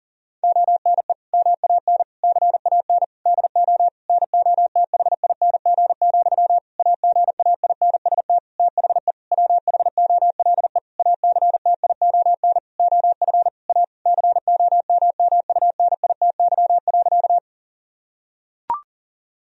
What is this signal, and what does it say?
Morse code, heard as a keyed tone, about 40 wpm, 700 Hz. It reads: ONE MAN CAN DO NOTHING, AGAINST THE WHOLE ACTION OF A COMMUNITY.
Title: Sentences from Top 1000 Words - Review Part 4 - 40wpm